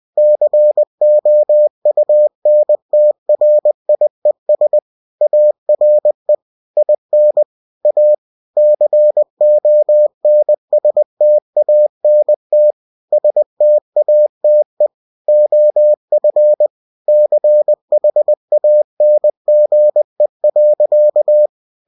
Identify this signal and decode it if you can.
Morse code, heard as a keyed tone, about 20 wpm, 600 Hz. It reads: COUNTRIES ARE IN A CONSTANT STATE OF CHANGE.
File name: cw-books / winnie-the-pooh / winnie-20wpm-page-0114